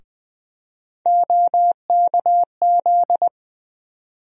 Morse code, heard as a keyed tone, about 20 wpm, 700 Hz. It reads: OKZ
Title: USA Call Sign Postfixes - 3 Letters 20wpm